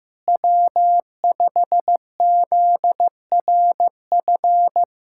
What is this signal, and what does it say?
Morse code, heard as a keyed tone, about 16 wpm, 700 Hz. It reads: W5ZRF
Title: K1USN SST QSO Practice - 15wpm